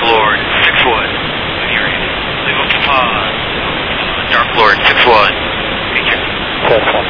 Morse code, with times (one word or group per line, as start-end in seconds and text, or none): none